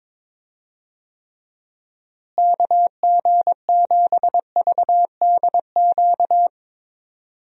Morse code, heard as a keyed tone, about 22 wpm, 700 Hz. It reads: KG7VDQ
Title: USA Call Signs - 6 Characters - Repeated 2x - 22wpm